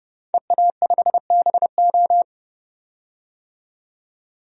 Morse code, heard as a keyed tone, about 31 wpm, 700 Hz. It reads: EA5BO
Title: DXCC-100 Call Signs - 5 Characters - 30wpm